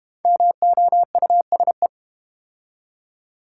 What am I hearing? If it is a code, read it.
Morse code, heard as a keyed tone, about 30 wpm, 700 Hz. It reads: MOUSE
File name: Easy Words - Speed-Racing with Hints - 25wpm